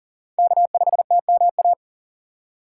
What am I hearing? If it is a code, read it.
Morse code, heard as a keyed tone, about 37 wpm, 700 Hz. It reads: K5TMA
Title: USA Call Signs - 5 Characters - 4x Character Spacing - 40wpm